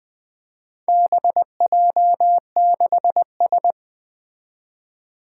Morse code, heard as a keyed tone, about 20 wpm, 700 Hz. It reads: BJ6S
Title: Four Challenging Letter-Number Combinations 20wpm